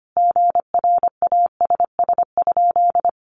Morse code, heard as a keyed tone, about 25 wpm, 700 Hz. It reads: GRASS?
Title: A Sound Of Thunder 25wpm